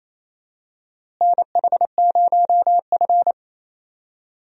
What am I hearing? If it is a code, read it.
Morse code, heard as a keyed tone, about 28 wpm, 700 Hz. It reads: NH0F
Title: USA Call Signs - 4 Characters - Repeated 2x - 28wpm